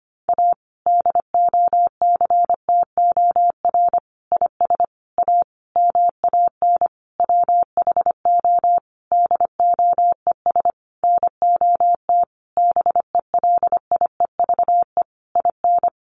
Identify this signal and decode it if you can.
Morse code, heard as a keyed tone, about 25 wpm, 700 Hz. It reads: A DOCTOR IS A MAN WHO DOES NOT BELIEVE IN
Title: Sentences from Top 1500 Words 25wpm